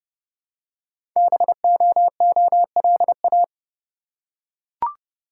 Morse code, heard as a keyed tone, about 30 wpm, 700 Hz. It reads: BOOLA
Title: Code Groups - 5 Letters - 30wpm